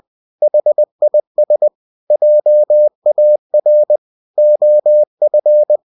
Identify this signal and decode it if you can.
Morse code, heard as a keyed tone, about 20 wpm, 600 Hz. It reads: HIS JAR OF